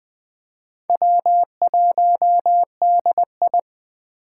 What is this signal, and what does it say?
Morse code, heard as a keyed tone, about 24 wpm, 700 Hz. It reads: W1DI
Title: Speed Racing - USA 1x2 Call Signs 20wpm